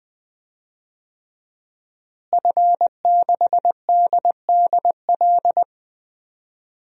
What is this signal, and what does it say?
Morse code, heard as a keyed tone, about 20 wpm, 700 Hz. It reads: F6DDL
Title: DXCC-100 Call Signs - 5 Characters - Repeated 3x - 20wpm